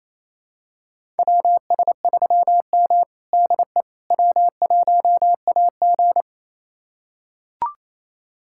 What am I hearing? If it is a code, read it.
Morse code, heard as a keyed tone, about 27 wpm, 700 Hz. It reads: WS3M DE W1AG
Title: USA Callsign DE Callsign - 4 Characters - 28wpm